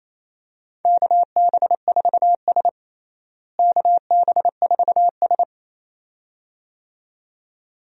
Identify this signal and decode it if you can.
Morse code, heard as a keyed tone, about 28 wpm, 700 Hz. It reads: KB4S KB4S
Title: USA Call Signs - 4 Characters - Repeated 2x - 28wpm